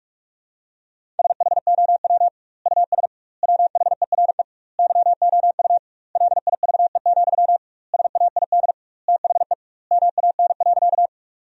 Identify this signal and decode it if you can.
Morse code, heard as a keyed tone, about 46 wpm, 700 Hz. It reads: SHOW US WHERE YOU LIVE, SAID THE MAN.
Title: 2x Word-Spacing - Sentences from Top 200 Words - 45wpm